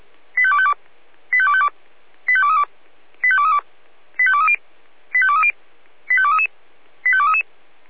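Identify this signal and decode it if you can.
Five-tone selective call (ZVEI), heard as quick runs of five tones, tones of about 70 ms. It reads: 85252 85252 85232 85232 85230 85230 85233 85233
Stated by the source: ZVEI five-tone selective call received on SDR.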